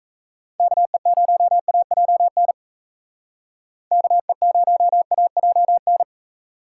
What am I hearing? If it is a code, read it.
Morse code, heard as a keyed tone, about 36 wpm, 700 Hz. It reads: KE0AJN KE0AJN